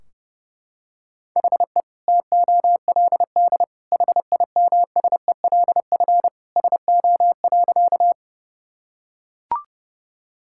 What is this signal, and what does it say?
Morse code, heard as a keyed tone, about 30 wpm, 700 Hz. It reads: HE TOLD HIMSELF SO.
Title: Sentences from Top 1600 Words 30wpm